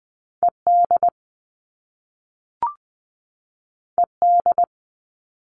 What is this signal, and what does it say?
Morse code, heard as a keyed tone, about 20 wpm, 700 Hz. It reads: ED ED
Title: All Cut Numbers - 2-Digit 20wpm